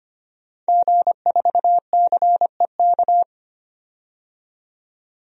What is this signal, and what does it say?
Morse code, heard as a keyed tone, about 25 wpm, 700 Hz. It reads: G4CEK
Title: DXCC-100 Call Signs - 5 Characters - 25wpm